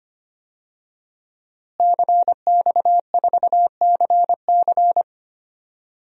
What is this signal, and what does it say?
Morse code, heard as a keyed tone, about 25 wpm, 700 Hz. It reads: CX4CC
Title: DXCC-100 Call Signs - 5 Characters - Repeated 2x - 25wpm